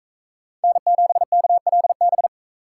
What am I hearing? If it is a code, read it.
Morse code, heard as a keyed tone, about 44 wpm, 700 Hz. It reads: N7KLB